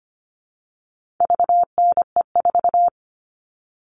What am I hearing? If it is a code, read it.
Morse code, heard as a keyed tone, about 25 wpm, 700 Hz. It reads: VNE4
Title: All Cut Numbers - 4-Digit 25wpm